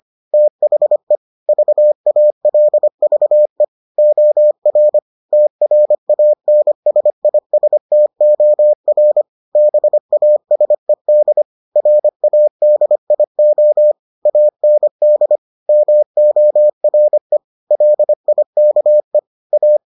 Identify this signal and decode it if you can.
Morse code, heard as a keyed tone, about 25 wpm, 600 Hz. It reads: THE VALVE OR TRANSISTOR BASED RADIO AND MORE LIKE A